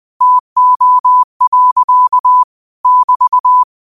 Morse code, heard as a keyed tone, about 20 wpm, 1000 Hz. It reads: TO. <BT>
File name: qotdfiles / 20wpm